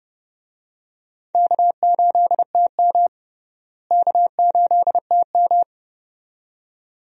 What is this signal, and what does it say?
Morse code, heard as a keyed tone, about 30 wpm, 700 Hz. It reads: K8TM K8TM
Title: USA Call Signs - 4 Characters - Repeated 2x - 30wpm